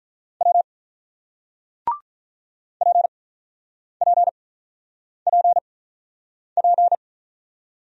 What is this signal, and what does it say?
Morse code, heard as a keyed tone, about 48 wpm, spaced as for 9 wpm, 700 Hz. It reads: W PPPP